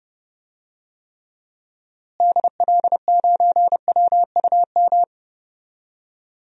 Morse code, heard as a keyed tone, about 30 wpm, 700 Hz. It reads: DL9WUM
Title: DXCC-100 Call Signs - 6 Characters - Repeated 2x - 30wpm